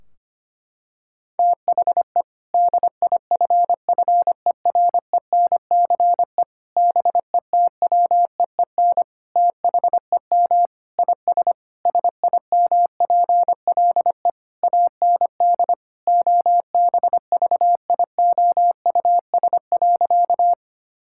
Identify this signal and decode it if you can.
Morse code, heard as a keyed tone, about 25 wpm, 700 Hz. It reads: THE DIFFERENCE BETWEEN THEM IS SIMPLE AND OBVIOUS.
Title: Sentences from Top 1600 Words 25wpm